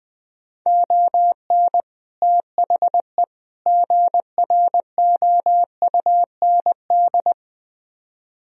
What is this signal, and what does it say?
Morse code, heard as a keyed tone, about 20 wpm, 700 Hz. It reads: ON THE GROUND
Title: Speed-Racing - Sets of 3 Words 17wpm